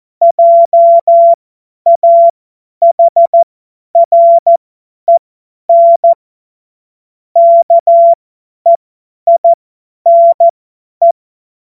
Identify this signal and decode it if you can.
Morse code, heard as a keyed tone, about 14 wpm, spaced as for 10 wpm, 680 Hz. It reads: JAHREN KEINE
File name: Witz-22-Juli-2025-50-F70